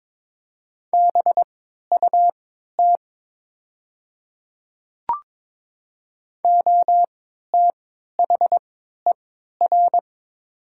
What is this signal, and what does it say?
Morse code, heard as a keyed tone, about 22 wpm, spaced as for 13 wpm, 700 Hz. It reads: BUT OTHER
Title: 3x Character Spacing - Top 100 Words - 22wpm